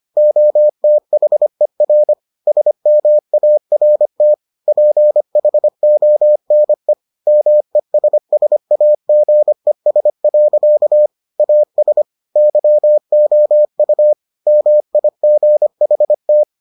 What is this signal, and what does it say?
Morse code, heard as a keyed tone, about 25 wpm, 600 Hz. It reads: OTHER SMART PHONE MESSAGES. AS YOU MIGHT